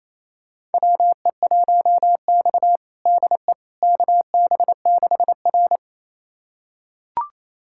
Morse code, heard as a keyed tone, about 28 wpm, 700 Hz. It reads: WE1X DE KB6R
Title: USA Callsign DE Callsign - 4 Characters - 28wpm